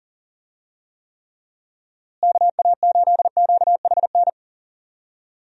Morse code, heard as a keyed tone, about 40 wpm, 700 Hz. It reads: KA8QHN
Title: USA Call Signs - 6 Characters - Repeated 3x - 40wpm